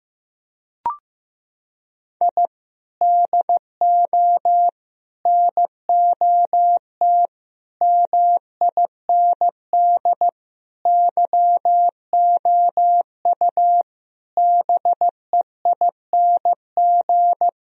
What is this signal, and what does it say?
Morse code, heard as a keyed tone, about 15 wpm, 700 Hz. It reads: I DO NOT MIND YOU BEING
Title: Sentences from Top 1000 Words - Review Part 3 - 15wpm